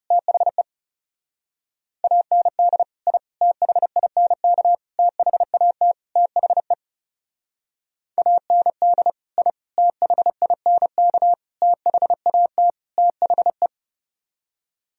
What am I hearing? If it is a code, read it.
Morse code, heard as a keyed tone, about 30 wpm, 700 Hz. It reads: THE AND I THINK THAT THE AND I THINK THAT THE
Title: Speed-Racing - Sets of 5 Words 30wpm